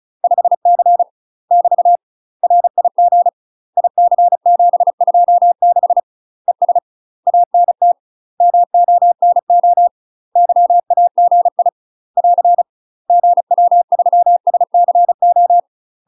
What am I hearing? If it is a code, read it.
Morse code, heard as a keyed tone, about 36 wpm, 700 Hz. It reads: 5C <BT> RIG IC726 ES ANT MONO YAGI <AR> GW3SCO
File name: UK MASTER QSOs TRAINING / 35WPM / UK MASTER QSOs 0035